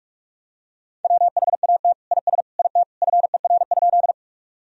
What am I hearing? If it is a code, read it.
Morse code, heard as a keyed tone, about 46 wpm, 700 Hz. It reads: WHAT IS IT FER?
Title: Sentences from Top 100 Words and Abbreviations 45wpm